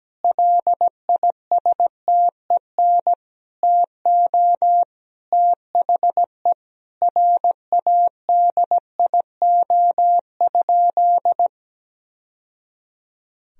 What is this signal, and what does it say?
Morse code, heard as a keyed tone, about 17 wpm, 700 Hz. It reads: LISTEN TO THE RADIO?
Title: Sentences from Top 1000 Words - Review Part 3 - 17wpm